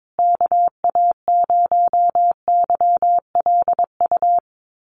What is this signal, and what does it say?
Morse code, heard as a keyed tone, about 22 wpm, 700 Hz. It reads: KA0YLU
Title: USA Call Signs Repeated 2x 22wpm